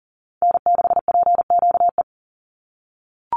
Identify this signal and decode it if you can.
Morse code, heard as a keyed tone, about 40 wpm, 700 Hz. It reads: N6PQE
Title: USA Call Signs 40wpm